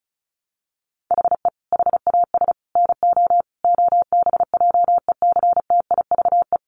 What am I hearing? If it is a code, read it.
Morse code, heard as a keyed tone, about 34 wpm, 700 Hz. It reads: HE HAS NO OBJECTIVE
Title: Sentences from Top 1200 Words 35wpm